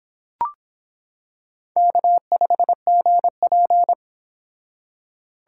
Morse code, heard as a keyed tone, about 24 wpm, 700 Hz. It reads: K5GP